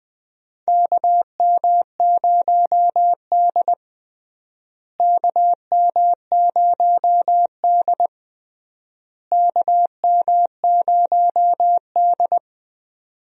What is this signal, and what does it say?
Morse code, heard as a keyed tone, about 20 wpm, 700 Hz. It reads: KM0D KM0D KM0D
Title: USA Call Signs - 4 Characters - Repeated 3x - 20wpm